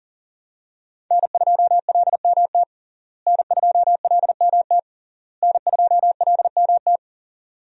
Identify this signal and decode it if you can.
Morse code, heard as a keyed tone, about 40 wpm, 690 Hz. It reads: N2LMT N2LMT N2LMT
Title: USA Call Signs - 5 Characters - Repeated 3x - 40wpm